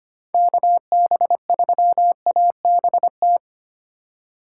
Morse code, heard as a keyed tone, about 25 wpm, 700 Hz. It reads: KB3ABT
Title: USA Call Signs - 3x Character Spacing - 25wpm